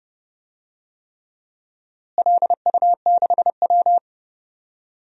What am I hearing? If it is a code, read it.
Morse code, heard as a keyed tone, about 30 wpm, 700 Hz. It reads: LU6W
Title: DXCC-100 Call Signs - 4 Characters - Repeated 3x - 30wpm